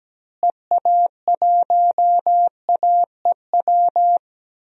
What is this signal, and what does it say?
Morse code, heard as a keyed tone, about 17 wpm, 700 Hz. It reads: EA1AEW
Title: DXCC-100 Call Signs - 6 Characters - Repeated 3x - 17wpm